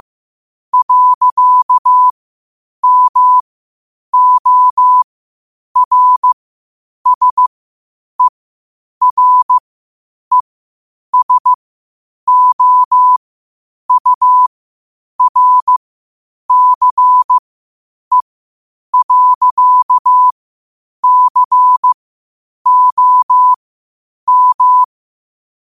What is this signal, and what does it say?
Morse code, heard as a keyed tone, about 15 wpm, spaced as for 8 wpm, 1000 Hz. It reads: .MORSERESOURCE.COM
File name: qotdfiles / 8wpm